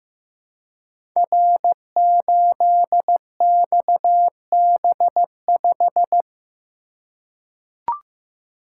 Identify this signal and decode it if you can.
Morse code, heard as a keyed tone, about 15 wpm, 700 Hz. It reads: R8XB5
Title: Code Groups - 5 Letter:Numbers - 15wpm